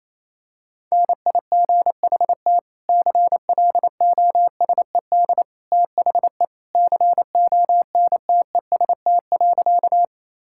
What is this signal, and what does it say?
Morse code, heard as a keyed tone, about 28 wpm, 700 Hz. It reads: NIGHT CLOSED THE CONTEST.
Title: Sentences from Top 1600 Words 28wpm